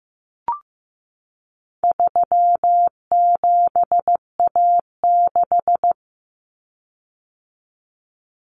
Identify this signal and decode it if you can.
Morse code, heard as a keyed tone, about 15 wpm, 700 Hz. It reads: 37A6
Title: Common Cut Numbers - 4-Digit 15wpm